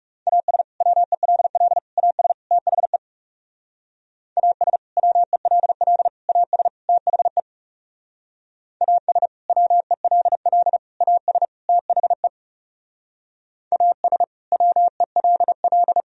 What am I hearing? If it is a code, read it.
Morse code, heard as a keyed tone, about 35 wpm, 700 Hz. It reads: AS WELL AS THE AS WELL AS THE AS WELL AS THE AS WELL
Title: Speed-Racing - Sets of 4 Words 30wpm